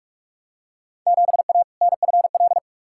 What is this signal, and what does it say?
Morse code, heard as a keyed tone, about 46 wpm, 700 Hz. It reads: 7A NFL
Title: Field Day Exchanges Encore - 45wpm